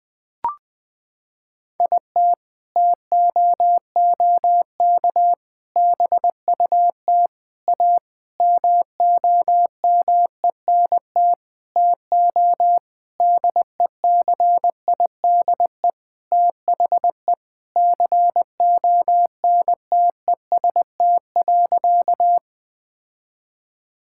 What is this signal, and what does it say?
Morse code, heard as a keyed tone, about 20 wpm, 700 Hz. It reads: IT TOOK BUT A MOMENT TO DECIDE THE CONTEST.
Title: Sentences from Top 1600 Words 20wpm